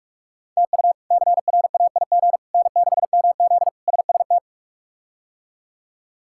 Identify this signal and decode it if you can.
Morse code, heard as a keyed tone, about 46 wpm, 700 Hz. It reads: TU CRAIG N6MZ SST
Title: K1USN SST QSO Practice - 45wpm